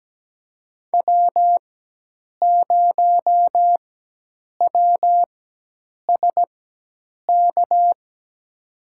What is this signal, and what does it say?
Morse code, heard as a keyed tone, about 17 wpm, spaced as for 8 wpm, 700 Hz. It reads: W0WSK
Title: USA Call Signs - 4x Character Spacing - 17wpm